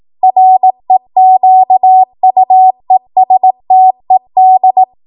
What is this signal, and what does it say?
Morse code, heard as a keyed tone, about 18 wpm, 750 Hz. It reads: REQUESTED
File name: Morse / 18WPM